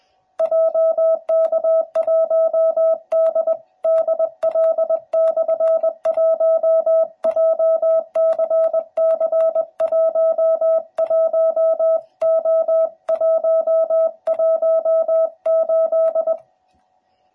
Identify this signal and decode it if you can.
Morse code, heard as a keyed tone, about 20 wpm, 650 Hz. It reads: JK1DDL/1 JCC11O118
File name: jh8cbh / 2021cw / cw12b